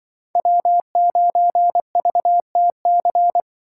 Morse code, heard as a keyed tone, about 24 wpm, 700 Hz. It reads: W9VTC
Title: Speed Racing - USA 1x3 Call Signs 20wpm